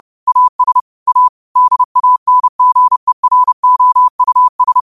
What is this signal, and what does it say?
Morse code, heard as a keyed tone, about 30 wpm, 1000 Hz. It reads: AS A DANGEROUS
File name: qotdfiles / 30wpm